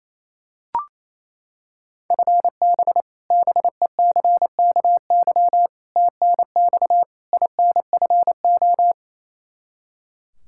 Morse code, heard as a keyed tone, about 28 wpm, 700 Hz. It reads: FB BECKY TNX INFO
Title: Practice QSO - Exchange 3 - More Info - 28wpm